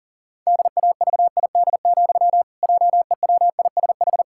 Morse code, heard as a keyed tone, about 37 wpm, 700 Hz. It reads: DAVID, JEWISH
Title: Pangram Challenge v2 - 40wpm